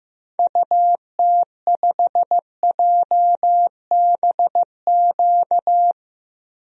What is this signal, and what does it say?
Morse code, heard as a keyed tone, about 16 wpm, 700 Hz. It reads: UT5JBQ
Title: DXCC-100 Call Signs - 6 Characters - 15wpm